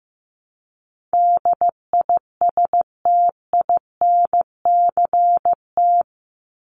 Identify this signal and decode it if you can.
Morse code, heard as a keyed tone, about 15 wpm, 700 Hz. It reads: DISTINCT
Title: Top 1501-1750 Words 15wpm